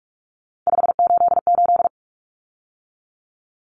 Morse code, heard as a keyed tone, about 44 wpm, 700 Hz. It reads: 588
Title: Numbers - 3-Digit 45wpm